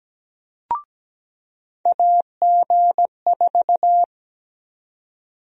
Morse code, heard as a keyed tone, about 17 wpm, 700 Hz. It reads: AG4
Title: USA Call Sign Prefixes - 17wpm